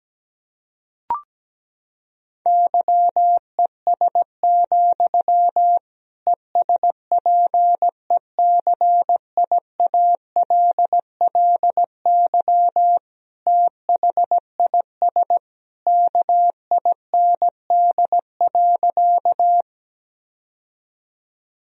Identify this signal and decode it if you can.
Morse code, heard as a keyed tone, about 17 wpm, 700 Hz. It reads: YES, ESPECIALLY THIS KIND.
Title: Sentences from Top 1000 Words - Review Part 4 - 17wpm